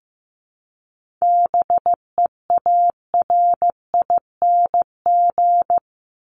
Morse code, heard as a keyed tone, about 15 wpm, 700 Hz. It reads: BEARING
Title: ING Suffix - Top 100 Words with Base Words 15wpm